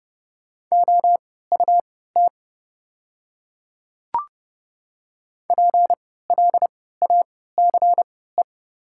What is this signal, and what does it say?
Morse code, heard as a keyed tone, about 30 wpm, spaced as for 17 wpm, 700 Hz. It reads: OUT PLACE
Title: 3x Character Spacing - Top 200 Words - 30wpm